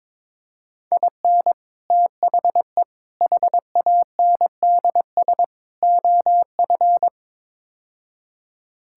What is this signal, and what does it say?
Morse code, heard as a keyed tone, about 22 wpm, 700 Hz. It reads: IN THE HANDS OF
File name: practice / 22wpm / Build A Phrase - 5-Word Phrases - 22wpm